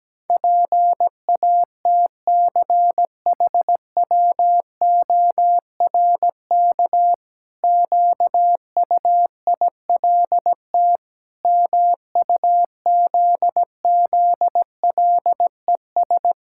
Morse code, heard as a keyed tone, about 17 wpm, 700 Hz. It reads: PATCHWORK QUILT MUZZLES
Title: Pangram Challenge v2 - 17wpm